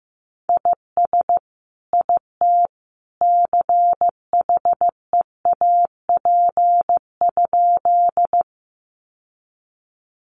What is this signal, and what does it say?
Morse code, heard as a keyed tone, about 15 wpm, 700 Hz. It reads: IS IT CHEAP?
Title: Sentences from Top 1200 Words 15wpm